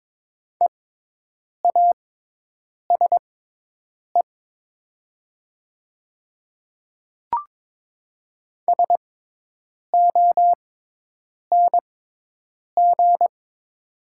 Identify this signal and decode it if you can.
Morse code, heard as a keyed tone, about 22 wpm, spaced as for 8 wpm, 700 Hz. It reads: EASE SONG
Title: 6x Character Spacing - Top 500 Words - 22wpm